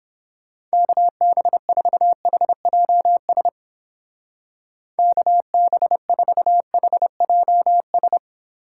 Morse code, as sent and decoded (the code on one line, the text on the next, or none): -.- -... ....- .... .--- ... / -.- -... ....- .... .--- ...
KB4HJS KB4HJS